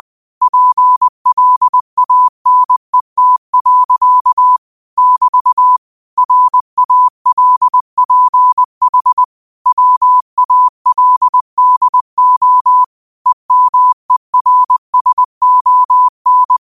Morse code, heard as a keyed tone, about 20 wpm, 1000 Hz. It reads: PLANET. <BT> RALPH WALDO EMERSON
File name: 20wpm